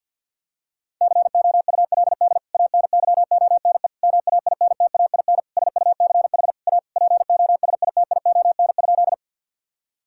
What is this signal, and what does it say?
Morse code, heard as a keyed tone, about 52 wpm, 700 Hz. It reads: COULD ANYONE MAINTAIN SUCH A POSITION?
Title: Sentences from Top 1000 Words - Review Part 5 - 50wpm